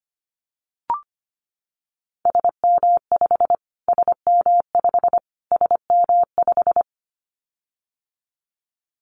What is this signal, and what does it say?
Morse code, heard as a keyed tone, about 25 wpm, 700 Hz. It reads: SM5 SM5 SM5
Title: Callsign Prefixes for Top 25 DXCC - Repeated 3x - 25wpm